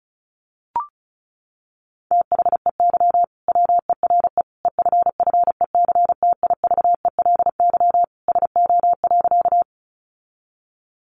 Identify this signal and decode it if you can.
Morse code, heard as a keyed tone, about 34 wpm, 700 Hz. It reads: THEY WERE EFFECTIVELY SO.
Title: Sentences from Top 1400 Words 35wpm